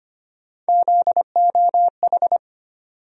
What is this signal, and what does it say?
Morse code, heard as a keyed tone, about 25 wpm, 700 Hz. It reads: ZOH
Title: USA Call Sign Postfixes - 3 Letters 25wpm